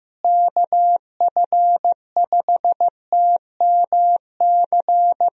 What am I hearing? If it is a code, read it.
Morse code, heard as a keyed tone, about 15 wpm, 700 Hz. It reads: KF5TMC
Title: USA Call Signs - 6 Characters - Repeated 3x - 15wpm